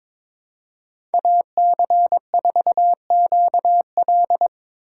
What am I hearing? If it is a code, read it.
Morse code, heard as a keyed tone, about 25 wpm, 700 Hz. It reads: AC4QL
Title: Speed Racing - USA 2x2 Call Signs 22wpm